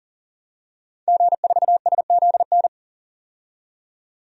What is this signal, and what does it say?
Morse code, heard as a keyed tone, about 40 wpm, 700 Hz. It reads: G4SZN
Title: DXCC-100 Call Signs - 5 Characters - 40wpm